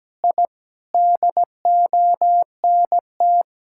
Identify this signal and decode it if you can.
Morse code, heard as a keyed tone, about 17 wpm, 700 Hz. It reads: I DONT
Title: Sentences from Top 1000 Words - Review Part 5 - 17wpm